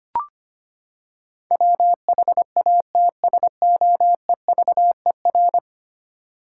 Morse code, heard as a keyed tone, about 25 wpm, 700 Hz. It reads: WHATSOEVER
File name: Ten Letter Words - 25wpm